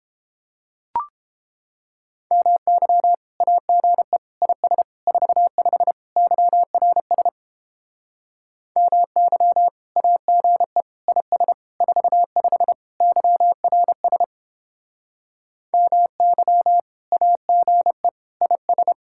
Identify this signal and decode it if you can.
Morse code, heard as a keyed tone, about 29 wpm, 700 Hz. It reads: MY AGE IS 45 YRS MY AGE IS 45 YRS MY AGE IS